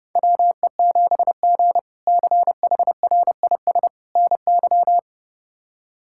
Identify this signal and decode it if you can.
Morse code, heard as a keyed tone, about 30 wpm, 700 Hz. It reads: WE7G CHRIS NY
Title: K1USN SST QSO Practice - 30wpm